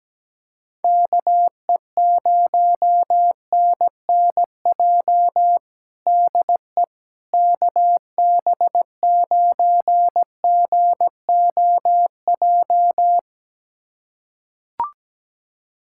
Morse code, heard as a keyed tone, about 17 wpm, 700 Hz. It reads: KE0NNJ DE KB9GOJ
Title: USA Callsign DE Callsign - 6 Characters - 17wpm